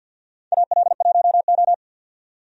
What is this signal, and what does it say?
Morse code, heard as a keyed tone, about 53 wpm, 700 Hz. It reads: AL1O
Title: USA Call Signs - 4 Characters - 50wpm